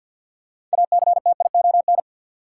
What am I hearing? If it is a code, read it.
Morse code, heard as a keyed tone, about 53 wpm, 700 Hz. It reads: ACTION
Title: Six Letter Words - 50wpm